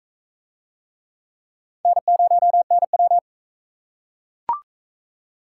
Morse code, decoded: N0NW

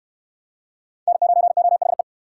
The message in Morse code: N9OHE

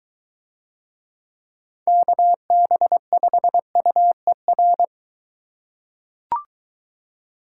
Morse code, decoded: KB5UER

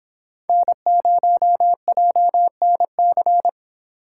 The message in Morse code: N0JNC

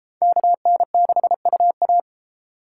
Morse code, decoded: KN6UA

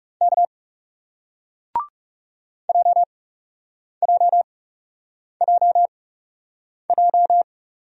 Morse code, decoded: K JJJJ

